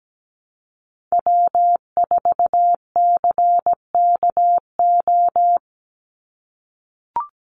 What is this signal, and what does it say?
Morse code, heard as a keyed tone, about 17 wpm, 700 Hz. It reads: W4CKO
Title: USA Call Signs 17wpm